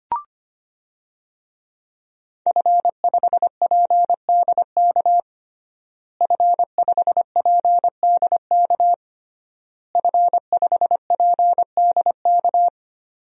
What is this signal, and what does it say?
Morse code, heard as a keyed tone, about 25 wpm, 700 Hz. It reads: F5PDK F5PDK F5PDK
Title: DXCC-100 Call Signs - 5 Characters - Repeated 3x - 25wpm